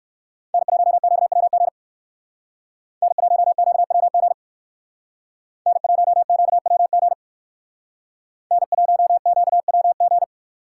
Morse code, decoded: N1QWG N1QWG N1QWG N1QWG

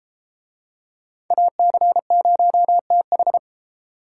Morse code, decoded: AC0TH

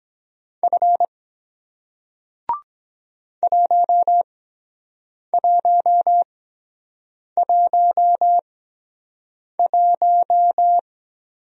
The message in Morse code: F 1111

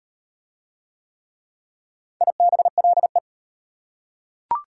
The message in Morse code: IBLE

